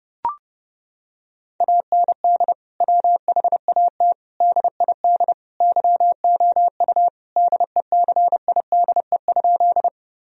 Morse code, heard as a keyed tone, about 30 wpm, 700 Hz. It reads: AND WHAT DID YOU DECIDE?
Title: Sentences from Top 1000 Words - Review Part 3 - 30wpm